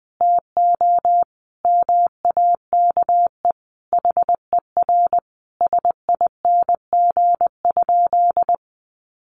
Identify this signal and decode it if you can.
Morse code, heard as a keyed tone, about 20 wpm, 700 Hz. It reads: TO MAKE HER SING?
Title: Sentences from Top 400 Words 20wpm